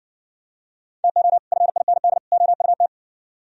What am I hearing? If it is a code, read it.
Morse code, heard as a keyed tone, about 54 wpm, 700 Hz. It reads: TO FIND OUT